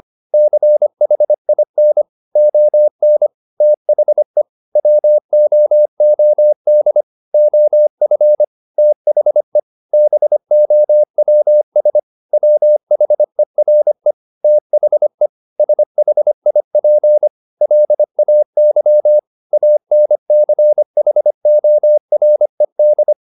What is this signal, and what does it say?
Morse code, heard as a keyed tone, about 25 wpm, 600 Hz. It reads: CHIN ON THE WOOD OF THE BOWS WHERE THE SHIP LAY ANCHORED